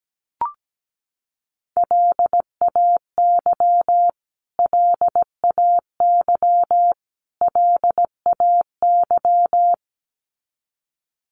Morse code, heard as a keyed tone, about 17 wpm, 700 Hz. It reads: LAY LAY LAY
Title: Top 500 Words Repeated 17wpm